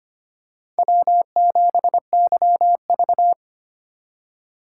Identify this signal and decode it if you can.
Morse code, heard as a keyed tone, about 25 wpm, 700 Hz. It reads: W7YV
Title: USA Call Signs - 4 Characters - 2x Character Spacing - 25wpm